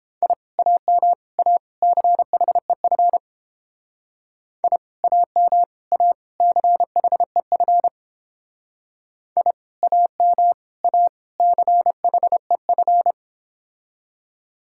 Morse code, heard as a keyed tone, about 29 wpm, 700 Hz. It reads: I AM A CHEF I AM A CHEF I AM A CHEF